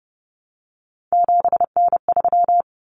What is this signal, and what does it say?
Morse code, heard as a keyed tone, about 30 wpm, 700 Hz. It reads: 7N3
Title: Common Cut Numbers - 3-Digit 30wpm